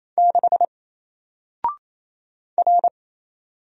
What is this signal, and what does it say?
Morse code, heard as a keyed tone, about 28 wpm, 700 Hz. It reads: 6 R
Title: Single Character (Letter:Number:Punctuation:Prosign) 28wpm